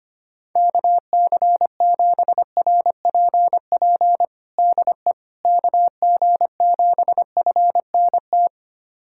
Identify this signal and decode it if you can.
Morse code, heard as a keyed tone, about 25 wpm, 700 Hz. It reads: KC7RPP DE KG7FNT
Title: USA Callsign DE Callsign - 6 Characters - 25wpm